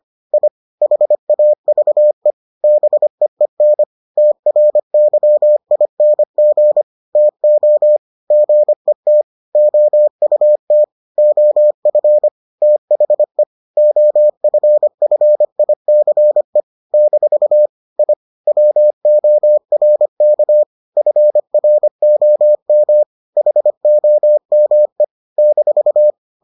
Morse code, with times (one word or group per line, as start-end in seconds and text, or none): 0.3-0.5 I
0.8-2.3 HAVE
2.6-3.8 BEEN
4.2-6.8 TRYING
7.2-8.0 TO
8.3-9.2 GET
9.6-10.8 OUT
11.2-12.3 OF
12.6-13.4 THE
13.8-16.6 OFFICE
16.9-17.7 -
18.0-18.1 I
18.5-20.6 WORK
21.0-23.0 FROM
23.4-25.0 HOME
25.4-26.1 -